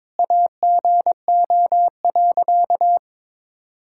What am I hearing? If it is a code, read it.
Morse code, heard as a keyed tone, about 22 wpm, 700 Hz. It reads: AGO.
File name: Sentences with 7-Word Phrases - 22wpm